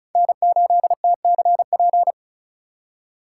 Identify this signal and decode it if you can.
Morse code, heard as a keyed tone, about 36 wpm, 700 Hz. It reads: N8TCP
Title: USA Call Signs - 5 Characters - 2x Character Spacing - 35wpm